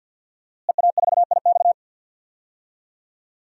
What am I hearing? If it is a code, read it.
Morse code, heard as a keyed tone, about 52 wpm, 700 Hz. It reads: EA4IX
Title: DXCC-100 Call Signs - 5 Characters - Repeated 3x - 50wpm